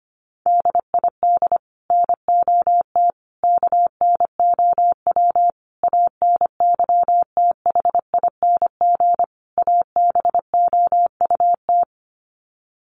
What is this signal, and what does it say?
Morse code, heard as a keyed tone, about 25 wpm, 700 Hz. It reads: DID NOT KNOW ANYTHING ABOUT
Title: Sets of 5 Words Encore2 25wpm